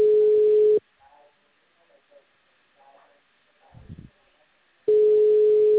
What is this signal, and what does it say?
Ringback tone (Europe, 425 Hz), with speech.